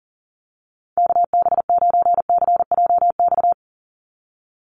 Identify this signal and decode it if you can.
Morse code, heard as a keyed tone, about 40 wpm, 700 Hz. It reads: KB9CJX
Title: USA Call Signs 40wpm